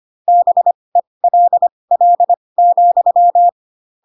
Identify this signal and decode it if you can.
Morse code, heard as a keyed tone, about 25 wpm, spaced as for 20 wpm, 700 Hz. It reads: BELL,
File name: Pinocchio / Pinocchio ch6 15 25 ik1hge